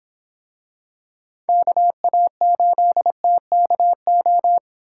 Morse code, heard as a keyed tone, about 26 wpm, 700 Hz. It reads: KA8TKO